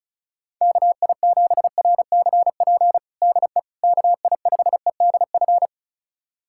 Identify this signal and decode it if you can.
Morse code, heard as a keyed tone, about 36 wpm, 700 Hz. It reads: KI7RCP DE KI5EDF
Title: USA Callsign DE Callsign 35wpm